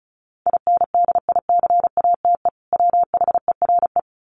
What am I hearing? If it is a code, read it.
Morse code, heard as a keyed tone, about 34 wpm, 700 Hz. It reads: INDICATE WHERE
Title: Sentences from Top 1300 Words 35wpm